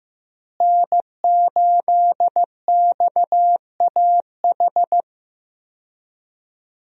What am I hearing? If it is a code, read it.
Morse code, heard as a keyed tone, about 16 wpm, 700 Hz. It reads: N8XAH
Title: USA Call Signs - 5 Characters - Repeated 3x - 15wpm